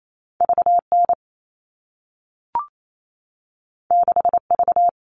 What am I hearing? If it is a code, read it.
Morse code, heard as a keyed tone, about 28 wpm, 700 Hz. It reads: VN 6V
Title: All Cut Numbers - 2-Digit 28wpm